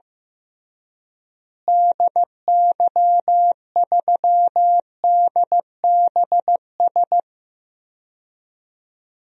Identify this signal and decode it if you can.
Morse code, heard as a keyed tone, about 15 wpm, 700 Hz. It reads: DY3DBS
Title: DXCC-100 Call Signs - 6 Characters - Repeated 2x - 15wpm